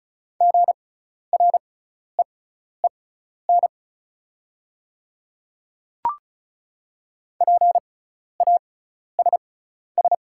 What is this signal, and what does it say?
Morse code, heard as a keyed tone, about 34 wpm, spaced as for 12 wpm, 690 Hz. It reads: GREEN PASS
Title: 6x Character Spacing - Top 500 Words - 35wpm